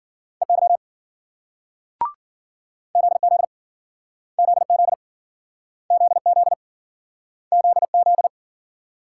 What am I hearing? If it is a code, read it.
Morse code, heard as a keyed tone, about 53 wpm, 700 Hz. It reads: EY ZZ ZZ ZZ ZZ